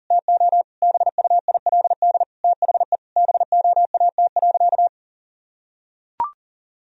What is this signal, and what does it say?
Morse code, heard as a keyed tone, about 40 wpm, 700 Hz. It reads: TO BUILD THE BOAT.
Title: Sentences from Top 1000 Words - Review Part 1 - 40wpm